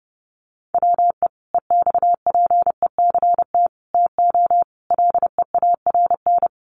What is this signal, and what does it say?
Morse code, heard as a keyed tone, about 30 wpm, 700 Hz. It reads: WE EXPECT TO LEARN
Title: Sentences from Top 1100 Words 30wpm